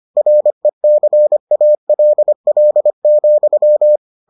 Morse code, heard as a keyed tone, about 25 wpm, 600 Hz. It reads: RECALL,